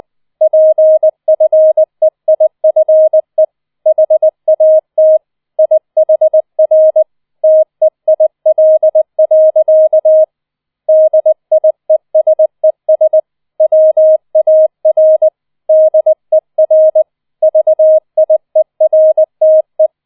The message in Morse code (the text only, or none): PFEIFE HAT IHR TEIL. DIESES WAR DER VIERTE